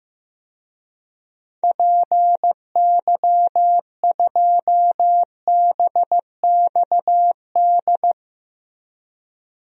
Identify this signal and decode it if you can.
Morse code, heard as a keyed tone, about 15 wpm, 700 Hz. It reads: PY2BXD
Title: DXCC-100 Call Signs - 6 Characters - Repeated 2x - 15wpm